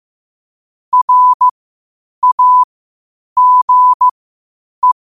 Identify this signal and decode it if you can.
Morse code, heard as a keyed tone, about 15 wpm, spaced as for 8 wpm, 1000 Hz. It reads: RAGE
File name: qotdfiles / 8wpm